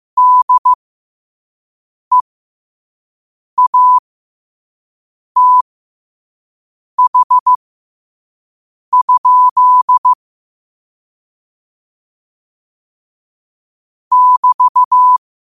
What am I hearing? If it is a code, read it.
Morse code, heard as a keyed tone, about 15 wpm, spaced as for 5 wpm, 1000 Hz. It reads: DEATH? <BT>
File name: qotdfiles / 5wpm